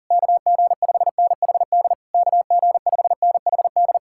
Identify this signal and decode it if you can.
Morse code, heard as a keyed tone, about 53 wpm, spaced as for 47 wpm, 700 Hz. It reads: KG5NHD KG5NHD
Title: Speed-Racing Common QSO Phrases - 40wpm